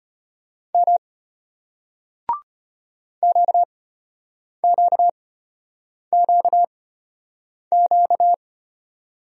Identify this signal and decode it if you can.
Morse code, heard as a keyed tone, about 34 wpm, spaced as for 8 wpm, 700 Hz. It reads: M QQQQ